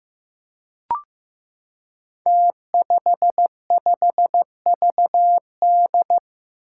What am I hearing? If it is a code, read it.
Morse code, heard as a keyed tone, about 15 wpm, 700 Hz. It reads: T55VD
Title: Five Challenging Letter-Number Combinations 15wpm